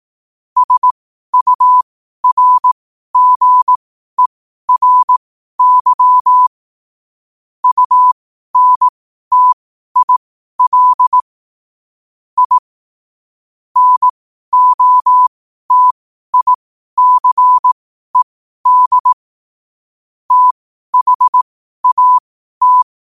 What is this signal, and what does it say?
Morse code, heard as a keyed tone, about 18 wpm, spaced as for 12 wpm, 1000 Hz. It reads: SURGERY UNTIL I NOTICED THAT
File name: qotdfiles / 12wpm